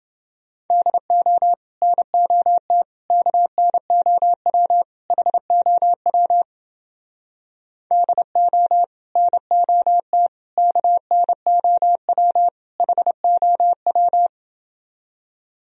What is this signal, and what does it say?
Morse code, heard as a keyed tone, about 30 wpm, 700 Hz. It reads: DO NOT KNOW HOW DO NOT KNOW HOW